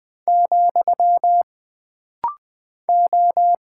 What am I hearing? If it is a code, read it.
Morse code, heard as a keyed tone, about 20 wpm, 700 Hz. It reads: , O
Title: Single Character (Letter:Number:Punctuation:Prosign) - Rapid-Fire - 20wpm